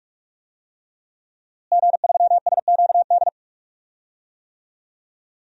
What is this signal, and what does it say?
Morse code, heard as a keyed tone, about 41 wpm, 700 Hz. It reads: G3SQD
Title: DXCC-100 Call Signs - 5 Characters - Repeated 3x - 45wpm